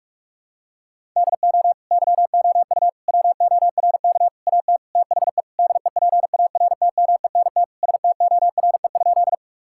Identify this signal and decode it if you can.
Morse code, heard as a keyed tone, about 46 wpm, 700 Hz. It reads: DO YOU WORK AT THE DEPARTMENT STORE?
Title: Sentences from Top 1000 Words - Review Part 5 - 45wpm